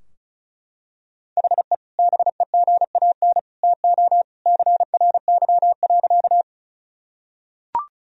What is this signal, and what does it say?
Morse code, heard as a keyed tone, about 34 wpm, 700 Hz. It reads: HE BEGAN TO CRY.
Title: Sentences from Top 1000 Words - Review Part 2 - 35wpm